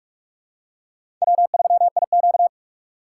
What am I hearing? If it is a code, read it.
Morse code, heard as a keyed tone, about 43 wpm, 700 Hz. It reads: W3IQ
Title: USA Call Signs - 4 Characters - 3x Character Spacing - 45wpm